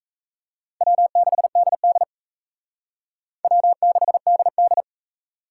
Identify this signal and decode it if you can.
Morse code, heard as a keyed tone, about 34 wpm, 700 Hz. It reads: W6DD W6DD
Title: Speed Racing - USA 1x2 Call Signs 28wpm